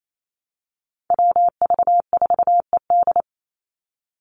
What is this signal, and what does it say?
Morse code, heard as a keyed tone, about 28 wpm, 700 Hz. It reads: WV4ED
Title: USA Call Signs 28wpm